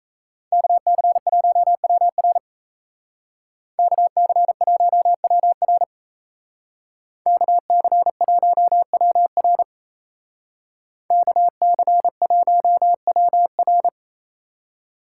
Morse code, heard as a keyed tone, about 37 wpm, 700 Hz. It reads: KC1WR KC1WR KC1WR KC1WR